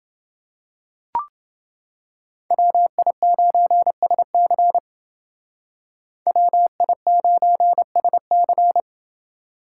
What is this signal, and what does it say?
Morse code, heard as a keyed tone, about 29 wpm, 700 Hz. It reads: WI9SC WI9SC